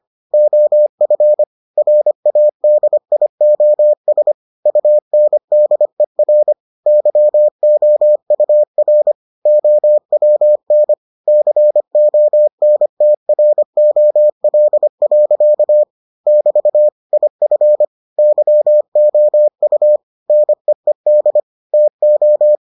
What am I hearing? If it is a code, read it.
Morse code, heard as a keyed tone, about 25 wpm, 600 Hz. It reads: OF RADIOS UNDER YOUR OWN CONTROL. <BT> IF YOU NEED TO